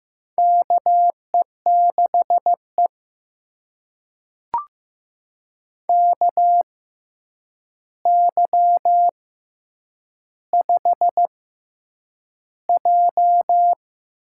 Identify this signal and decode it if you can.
Morse code, heard as a keyed tone, about 15 wpm, spaced as for 8 wpm, 700 Hz. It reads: KE6E KY5J
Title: USA Call Signs - 4 Characters - 6x Character Spacing - 15wpm